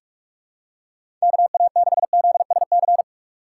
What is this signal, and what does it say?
Morse code, heard as a keyed tone, about 46 wpm, 690 Hz. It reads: KA6ZSC
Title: USA Call Signs - 6 Characters - 3x Character Spacing - 45wpm